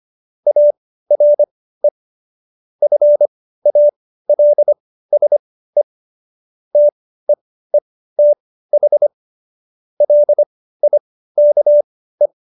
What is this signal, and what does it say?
Morse code, heard as a keyed tone, about 25 wpm, spaced as for 15 wpm, 590 Hz. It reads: ARE FALSE TEETH LIKE
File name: Jokes-25-char-15-eff